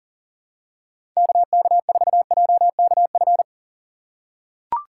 Morse code, heard as a keyed tone, about 40 wpm, 700 Hz. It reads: KK4JKF
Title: USA Call Signs - 6 Characters - 6x Character Spacing - 40wpm